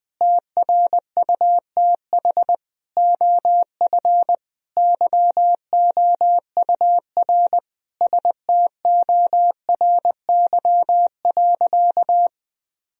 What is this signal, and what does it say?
Morse code, heard as a keyed tone, about 20 wpm, 700 Hz. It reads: TRUTH OF YOUR STORY.
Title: Sentences from Top 1000 Words - Review Part 3 - 20wpm